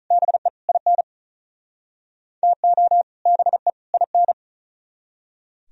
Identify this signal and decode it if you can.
Morse code, heard as a keyed tone, about 45 wpm, 700 Hz. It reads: BE IN TO BE IN